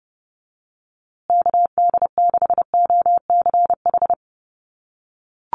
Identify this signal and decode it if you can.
Morse code, heard as a keyed tone, about 29 wpm, 700 Hz. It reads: KD6OCH
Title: USA Call Signs Repeated 3x 30wpm